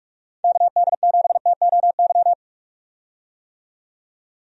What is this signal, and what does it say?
Morse code, heard as a keyed tone, about 46 wpm, 700 Hz. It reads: KD7TOY
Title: USA Call Signs - 6 Characters - 45wpm